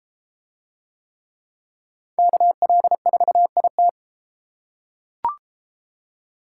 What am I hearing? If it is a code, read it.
Morse code, heard as a keyed tone, about 30 wpm, 700 Hz. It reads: KL4IT